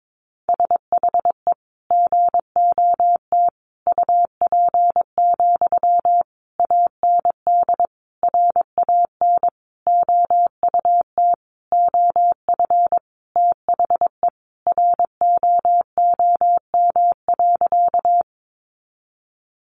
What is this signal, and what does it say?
Morse code, heard as a keyed tone, about 22 wpm, 700 Hz. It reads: SHE GOT UP, AND RAN OUT OF THE ROOM.
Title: Sentences from Top 500 Words 22wpm